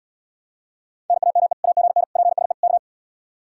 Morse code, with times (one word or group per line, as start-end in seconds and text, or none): 1.1-1.5 NAME
1.6-2.0 AGN
2.2-2.5 PSE
2.6-2.8 K